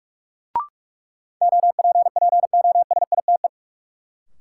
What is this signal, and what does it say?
Morse code, heard as a keyed tone, about 41 wpm, 700 Hz. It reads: OPPOSITE
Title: Sentences from Top 1300 Words - New Word Drill Alt - 45wpm